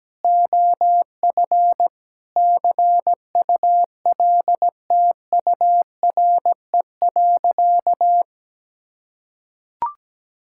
Morse code, heard as a keyed tone, about 17 wpm, 700 Hz. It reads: OF CULTURE.
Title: Sentences from Top 1000 Words - Review Part 6 - 17wpm